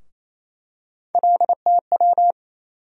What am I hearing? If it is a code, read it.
Morse code, heard as a keyed tone, about 28 wpm, 700 Hz. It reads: LTW
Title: USA Call Sign Postfixes - 3 Letters 28wpm